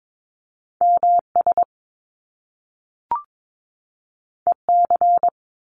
Morse code, heard as a keyed tone, about 22 wpm, 700 Hz. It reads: MS EC
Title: Top Two Letter Combinations in Words 22wpm